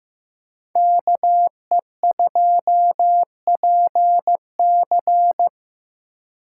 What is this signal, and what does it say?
Morse code, heard as a keyed tone, about 18 wpm, 700 Hz. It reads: KE2PC
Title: Speed Racing - USA 2x2 Call Signs 15wpm